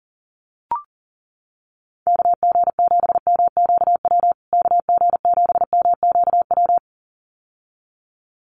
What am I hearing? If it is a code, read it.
Morse code, heard as a keyed tone, about 40 wpm, 700 Hz. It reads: KG7MQW KG7MQW
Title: USA Call Signs Repeated 2x 40wpm